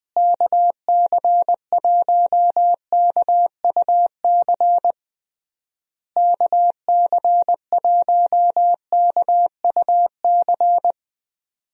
Morse code, heard as a keyed tone, about 20 wpm, 700 Hz. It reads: KC1KUC KC1KUC
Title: USA Call Signs - 6 Characters - Repeated 3x - 20wpm